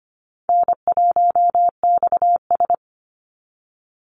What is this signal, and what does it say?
Morse code, heard as a keyed tone, about 25 wpm, 700 Hz. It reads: N1XS
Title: Easy Call Signs Repeated 2x - 25wpm